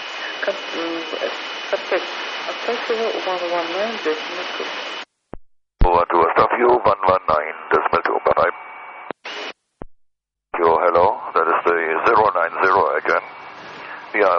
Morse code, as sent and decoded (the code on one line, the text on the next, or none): none